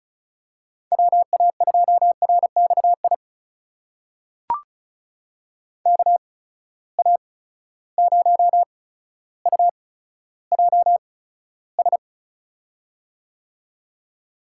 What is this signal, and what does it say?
Morse code, heard as a keyed tone, about 36 wpm, spaced as for 15 wpm, 700 Hz. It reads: WA2RXI KA0UJS
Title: USA Call Signs - 6 Characters - 8x Character Spacing - 35wpm